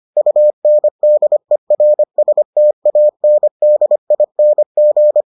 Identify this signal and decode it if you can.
Morse code, heard as a keyed tone, about 25 wpm, 600 Hz. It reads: UNDERSTANDING